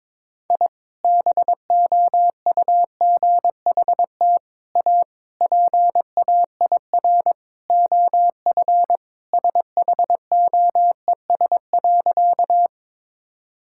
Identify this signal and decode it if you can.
Morse code, heard as a keyed tone, about 22 wpm, 700 Hz. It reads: I BOUGHT A PAIR OF SHOES.
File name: practice / 22wpm / Sentences with 3-Word Phrases - 22wpm